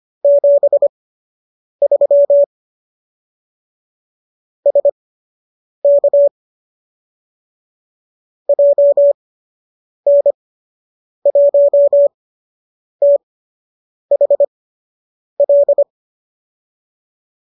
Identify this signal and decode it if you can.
Morse code, heard as a keyed tone, about 25 wpm, spaced as for 8 wpm, 570 Hz. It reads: 73 SK JN1THL
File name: qso003 08